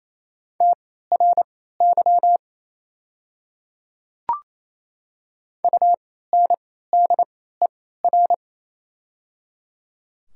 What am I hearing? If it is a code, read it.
Morse code, heard as a keyed tone, about 28 wpm, spaced as for 16 wpm, 700 Hz. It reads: TRY UNDER
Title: 3x Character Spacing - Top 200 Words - 28wpm